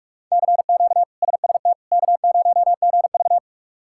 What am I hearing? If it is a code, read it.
Morse code, heard as a keyed tone, about 42 wpm, 700 Hz. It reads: CQ SST K0GV
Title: K1USN SST QSO Practice - 45wpm